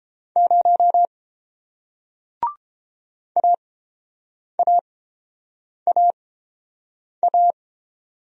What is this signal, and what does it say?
Morse code, heard as a keyed tone, about 30 wpm, spaced as for 7 wpm, 700 Hz. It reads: 0 AAAA